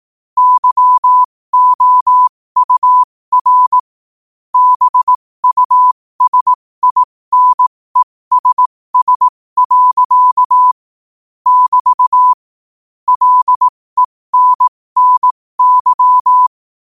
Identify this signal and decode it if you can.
Morse code, heard as a keyed tone, about 18 wpm, spaced as for 15 wpm, 1000 Hz. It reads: YOUR BUSINESS. <BT> LENNY
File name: qotdfiles / 15wpm